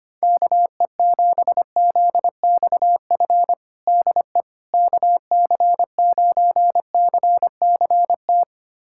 Morse code, heard as a keyed tone, about 25 wpm, 700 Hz. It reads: KE7ZXF DE KC9CCT
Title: USA Callsign DE Callsign - 6 Characters - 25wpm